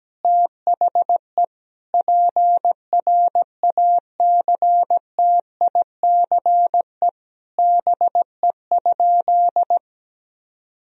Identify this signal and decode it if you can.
Morse code, heard as a keyed tone, about 17 wpm, 700 Hz. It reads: THE PRACTICE BE?
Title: Sentences from Top 1000 Words - Review Part 4 - 17wpm